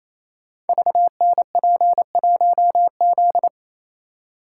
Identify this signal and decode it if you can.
Morse code, heard as a keyed tone, about 27 wpm, 700 Hz. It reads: VNP1Z
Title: Code Groups - 5 Letter:Numbers - 28wpm